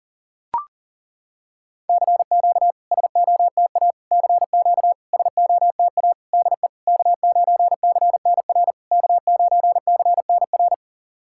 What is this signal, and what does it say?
Morse code, heard as a keyed tone, about 40 wpm, 690 Hz. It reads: CQ SOTA CQ SOTA DE K9CNR K9CNR
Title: SOTA QSO Practice 40wpm